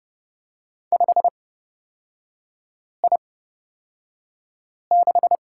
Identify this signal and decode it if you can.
Morse code, heard as a keyed tone, about 31 wpm, spaced as for 5 wpm, 700 Hz. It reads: 5I6
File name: Lesson 36e - BT - Three Characters 30wpm